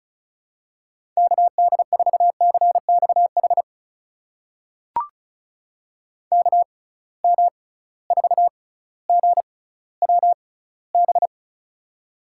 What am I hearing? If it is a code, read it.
Morse code, heard as a keyed tone, about 36 wpm, spaced as for 18 wpm, 690 Hz. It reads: KD4CXH KM4GWB
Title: USA Call Signs - 6 Characters - 6x Character Spacing - 35wpm